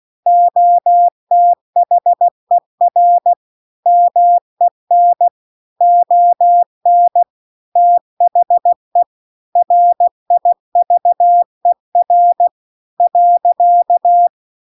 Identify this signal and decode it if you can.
Morse code, heard as a keyed tone, about 16 wpm, 700 Hz. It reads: OTHER MEN ON THE RIVER .
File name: CW-The-Big-Twoheaded-River-2-16WPM